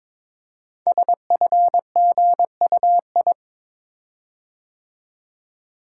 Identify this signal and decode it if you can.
Morse code, heard as a keyed tone, about 22 wpm, 700 Hz. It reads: SFGUI
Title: Code Groups - 5 Letter:Numbers - 22wpm